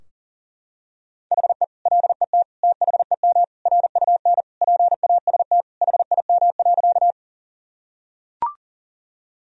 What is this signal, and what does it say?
Morse code, heard as a keyed tone, about 40 wpm, 700 Hz. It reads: HE LET THEM RUN PAST HIM.
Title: Sentences from Top 1000 Words - Review Part 3 - 40wpm